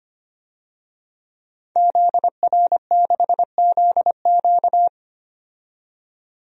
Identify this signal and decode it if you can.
Morse code, heard as a keyed tone, about 25 wpm, 700 Hz. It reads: ZR6ZQ
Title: DXCC-100 Call Signs - 5 Characters - 25wpm